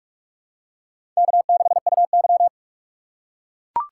Morse code, heard as a keyed tone, about 44 wpm, 690 Hz. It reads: K6UY
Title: USA Call Signs - 4 Characters - 2x Character Spacing - 45wpm